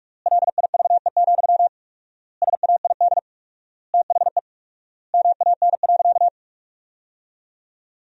Morse code, heard as a keyed tone, about 46 wpm, 700 Hz. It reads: LIVE, SAID THE MAN.
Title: 4x Word-Spacing - Sentences from Top 200 Words - 45wpm